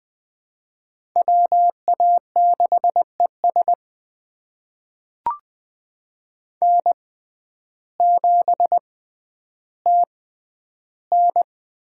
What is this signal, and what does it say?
Morse code, heard as a keyed tone, about 20 wpm, spaced as for 11 wpm, 700 Hz. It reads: WA6ES N7TN
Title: USA Call Signs - 6x Character Spacing - 20wpm